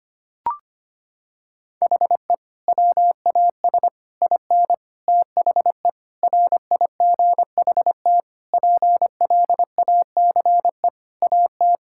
Morse code, heard as a keyed tone, about 25 wpm, 700 Hz. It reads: HE WAS IN THE RIGHT PLACE AT
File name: practice / 25wpm / Sentences with 9-Word Phrases - 25wpm